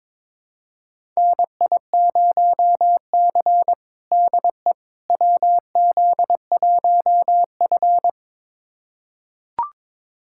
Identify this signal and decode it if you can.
Morse code, heard as a keyed tone, about 21 wpm, 700 Hz. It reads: NI0C DE WZ1F
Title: USA Callsign DE Callsign - 4 Characters - 22wpm